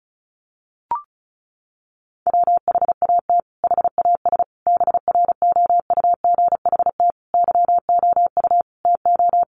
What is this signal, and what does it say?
Morse code, heard as a keyed tone, about 34 wpm, 700 Hz. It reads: WHAT HAS BROUGHT YOU TO
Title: Sentences from Top 1500 Words 35wpm